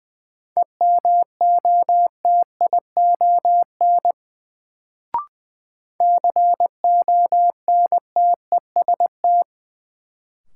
Morse code, heard as a keyed tone, about 20 wpm, 700 Hz. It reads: EMOTION CONTEST
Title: Sentences from Top 1600 Words - New Word Drill - 20wpm